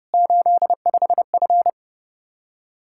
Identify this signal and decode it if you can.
Morse code, heard as a keyed tone, about 30 wpm, 700 Hz. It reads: 85F
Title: Practice QSO - Exchange 2 - Info - 30wpm